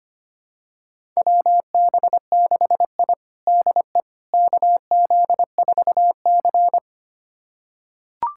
Morse code, heard as a keyed tone, about 25 wpm, 700 Hz. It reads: WB6I DE KZ4C
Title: USA Callsign DE Callsign - 4 Characters - 25wpm